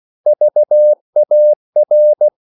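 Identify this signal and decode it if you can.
Morse code, heard as a keyed tone, about 16 wpm, 600 Hz. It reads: VAR
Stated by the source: Slow (QRS) Morse code at 16 wpm, from CW med Gnister.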